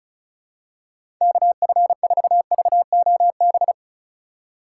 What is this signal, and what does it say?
Morse code, heard as a keyed tone, about 36 wpm, 700 Hz. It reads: KF4VOB
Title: USA Call Signs - 6 Characters - 4x Character Spacing - 35wpm